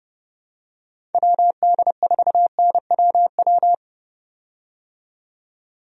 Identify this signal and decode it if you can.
Morse code, heard as a keyed tone, about 30 wpm, 700 Hz. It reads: WD4NWW
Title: USA Call Signs - 6 Characters - 30wpm